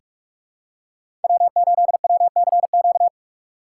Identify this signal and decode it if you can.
Morse code, heard as a keyed tone, about 46 wpm, 700 Hz. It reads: W8WCQ
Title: USA Call Signs - 5 Characters - 2x Character Spacing - 45wpm